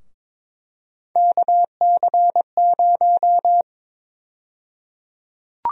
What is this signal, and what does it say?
Morse code, heard as a keyed tone, about 22 wpm, 700 Hz. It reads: KC0
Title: USA Call Sign Prefixes - 22wpm